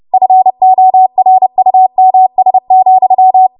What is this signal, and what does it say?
Morse code, heard as a keyed tone, about 30 wpm, 750 Hz. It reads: FORUMS,
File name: Morse / 30WPM